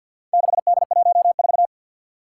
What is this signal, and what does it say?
Morse code, heard as a keyed tone, about 52 wpm, 700 Hz. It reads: 6D14
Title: Four Challenging Letter-Number Combinations 50wpm